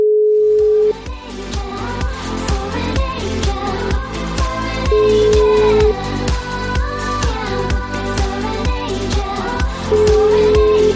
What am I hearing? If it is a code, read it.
Ringback tone (Europe, 425 Hz), with noise.